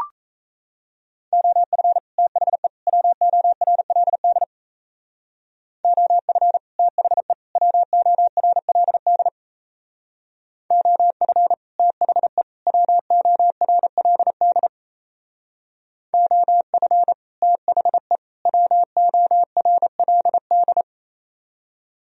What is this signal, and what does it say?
Morse code, heard as a keyed tone, about 35 wpm, 700 Hz. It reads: OF THE WORLD OF THE WORLD OF THE WORLD OF THE WORLD
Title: Speed-Racing - Sets of 3 Words 28wpm